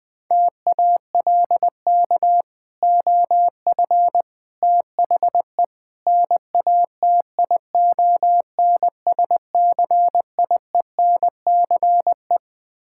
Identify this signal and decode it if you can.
Morse code, heard as a keyed tone, about 20 wpm, 700 Hz. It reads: TALK OF THE NATION SCIENCE
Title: Build A Phrase - 5-Word Phrases - 20wpm